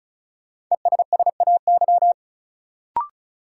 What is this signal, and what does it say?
Morse code, heard as a keyed tone, about 34 wpm, 700 Hz. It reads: ESSAY
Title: Sentences from Top 1500 Words - New Word Drill - 35wpm